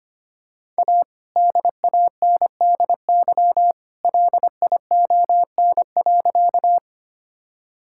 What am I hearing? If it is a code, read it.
Morse code, heard as a keyed tone, about 25 wpm, 700 Hz. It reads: A DANDY LION.
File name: practice / 25wpm / All-Age Appropriate Jokes - 25wpm